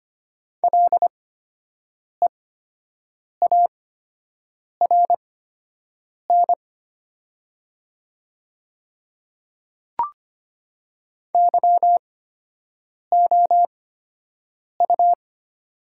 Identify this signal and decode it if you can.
Morse code, heard as a keyed tone, about 25 wpm, spaced as for 7 wpm, 700 Hz. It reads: LEARN YOU
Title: 8x Character Spacing - Top 300 Words - 25wpm